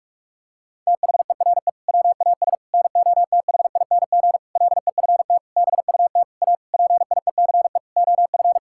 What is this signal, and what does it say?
Morse code, heard as a keyed tone, about 46 wpm, 700 Hz. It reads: THERE WAS NOTHING LEFT BUT A PIECE OF
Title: Sentences from Top 1600 Words 45wpm